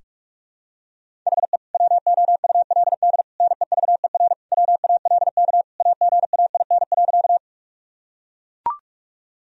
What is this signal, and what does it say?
Morse code, heard as a keyed tone, about 41 wpm, 700 Hz. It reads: HE WOULD NEVER WALK AGAIN.
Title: Sentences from Top 1000 Words - Review Part 1 - 45wpm